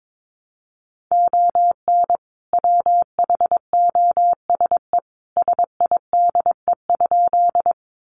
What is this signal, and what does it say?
Morse code, heard as a keyed tone, about 22 wpm, 700 Hz. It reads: ON WHOSE SIDE?
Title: Sentences from Top 1200 Words 22wpm